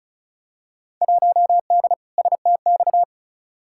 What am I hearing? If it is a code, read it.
Morse code, heard as a keyed tone, about 36 wpm, 700 Hz. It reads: 1D STX
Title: Field Day Exchanges Encore - 35wpm